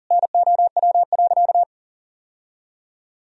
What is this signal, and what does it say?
Morse code, heard as a keyed tone, about 40 wpm, 700 Hz. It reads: NOW.
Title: Sentences from Top 1000 Words - Review Part 1 - 40wpm